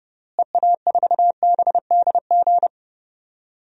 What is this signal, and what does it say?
Morse code, heard as a keyed tone, about 31 wpm, 700 Hz. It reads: EA4BDG
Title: DXCC-100 Call Signs - 6 Characters - Repeated 3x - 30wpm